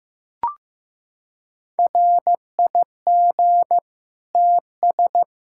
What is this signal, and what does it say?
Morse code, heard as a keyed tone, about 16 wpm, 700 Hz. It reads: RIG TS
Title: Practice QSO - Exchange 2 - Info - 15wpm